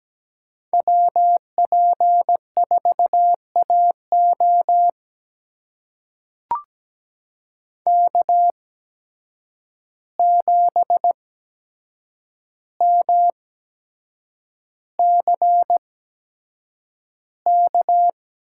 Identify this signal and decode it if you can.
Morse code, heard as a keyed tone, about 17 wpm, spaced as for 7 wpm, 700 Hz. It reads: WP4AO K7MCK
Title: USA Call Signs - 5 Characters - 8x Character Spacing - 17wpm